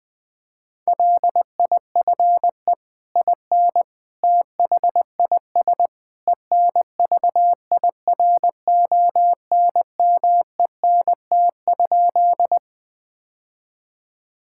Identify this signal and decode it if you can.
Morse code, heard as a keyed tone, about 20 wpm, 700 Hz. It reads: LIFE IN THIS ENVIRONMENT?
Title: Sentences from Top 1000 Words - Review Part 5 - 20wpm